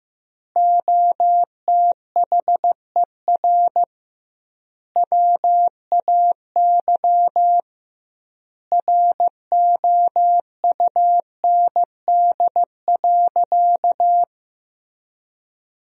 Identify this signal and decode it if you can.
Morse code, heard as a keyed tone, about 15 wpm, 700 Hz. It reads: OTHER WAY ROUND.
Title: 2x Word-Spacing - Sentences from Top 200 Words - 15wpm